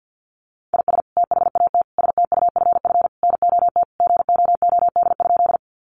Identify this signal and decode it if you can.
Morse code, heard as a keyed tone, about 51 wpm, 700 Hz. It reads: IS THAT STUFF NOT GOOD?
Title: Sentences from Top 1400 Words 50wpm